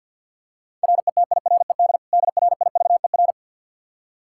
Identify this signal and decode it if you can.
Morse code, heard as a keyed tone, about 46 wpm, 700 Hz. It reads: RETIRED DRIVER
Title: Practice QSO - Exchange 3 - More Info - 50wpm